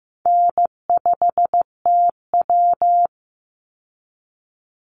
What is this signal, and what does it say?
Morse code, heard as a keyed tone, about 15 wpm, 700 Hz. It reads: N5TW
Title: Easy Call Signs Repeated 3x - 15wpm